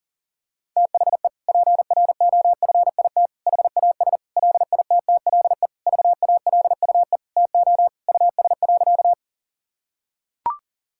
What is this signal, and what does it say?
Morse code, heard as a keyed tone, about 37 wpm, 700 Hz. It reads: THE PROFIT HAS LITTLE VALUE TO US.
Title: Sentences from Top 1000 Words - Review Part 5 - 40wpm